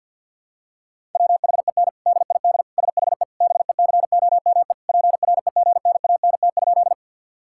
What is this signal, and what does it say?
Morse code, heard as a keyed tone, about 51 wpm, 700 Hz. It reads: WHEN DID SHE BECOME PREGNANT?
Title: Sentences from Top 1000 Words - Review Part 6 - 50wpm